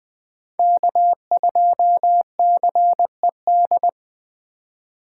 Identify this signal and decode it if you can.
Morse code, heard as a keyed tone, about 21 wpm, 700 Hz. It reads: K2CED